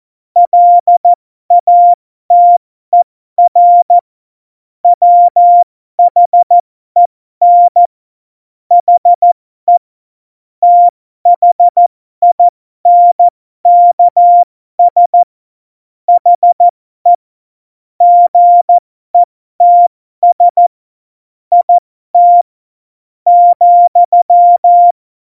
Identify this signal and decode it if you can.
Morse code, heard as a keyed tone, about 14 wpm, spaced as for 12 wpm, 700 Hz. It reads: LATER WHEN HE THINKS HE GETS IT ,
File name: Morsekurs / Witze